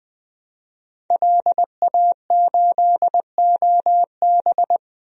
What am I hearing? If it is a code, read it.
Morse code, heard as a keyed tone, about 21 wpm, 700 Hz. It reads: LA8OB
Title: DXCC-100 Call Signs - 5 Characters - 20wpm